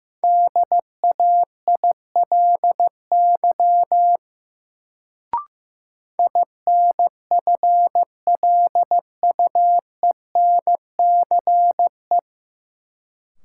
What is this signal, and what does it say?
Morse code, heard as a keyed tone, about 15 wpm, 700 Hz. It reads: DAILY INFLUENCE
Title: Sentences from Top 1100 Words - New Word Drill - 15wpm